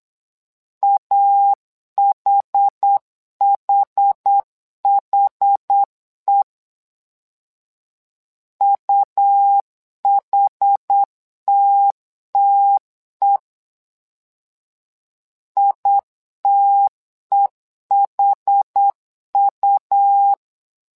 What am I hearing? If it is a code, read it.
Morse code, heard as a keyed tone, about 8 wpm, 800 Hz. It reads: AHHHE UHTTE ITEHU
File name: LEZIONI / 2 UHDN / 2 UH 8wpm 2129